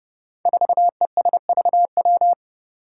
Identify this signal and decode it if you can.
Morse code, heard as a keyed tone, about 30 wpm, 700 Hz. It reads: 4ESVW
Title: Five Challenging Letter-Number Combinations 30wpm